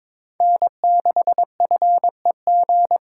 Morse code, heard as a keyed tone, about 26 wpm, 700 Hz. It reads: N6FEG
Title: Speed Racing - USA 1x3 Call Signs 22wpm